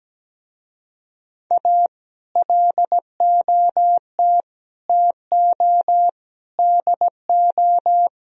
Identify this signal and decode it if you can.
Morse code, heard as a keyed tone, about 17 wpm, 700 Hz. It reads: A LOT TO DO
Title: Speed-Racing - Sets of 5 Words 17wpm